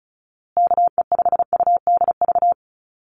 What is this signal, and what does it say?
Morse code, heard as a keyed tone, about 34 wpm, 700 Hz. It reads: KE5UDV
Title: USA Call Signs Repeated 2x 35wpm